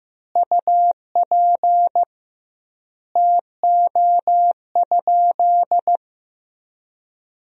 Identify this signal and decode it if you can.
Morse code, heard as a keyed tone, about 15 wpm, 700 Hz. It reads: UP TO?
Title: 2x Word-Spacing - Sentences from Top 100 Words - 15wpm